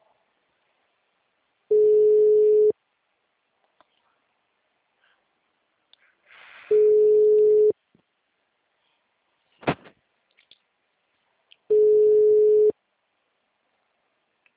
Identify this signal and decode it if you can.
Ringback tone (Europe, 425 Hz).
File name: out- -2018- - - .18